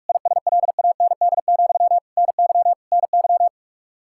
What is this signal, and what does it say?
Morse code, heard as a keyed tone, about 45 wpm, 700 Hz. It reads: ISLAND, NY NY
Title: Practice QSO - Exchange 1 - Beginning - 45wpm